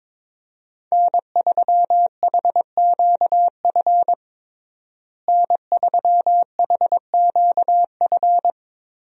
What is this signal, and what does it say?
Morse code, heard as a keyed tone, about 22 wpm, 700 Hz. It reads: N3HQF N3HQF
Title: USA Call Signs - 5 Characters - Repeated 2x - 22wpm